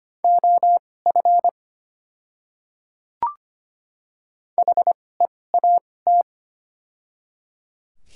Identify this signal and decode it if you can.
Morse code, heard as a keyed tone, about 25 wpm, spaced as for 18 wpm, 700 Hz. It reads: OF HEAT
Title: 2x Character Spacing - Top 500 Words - 25wpm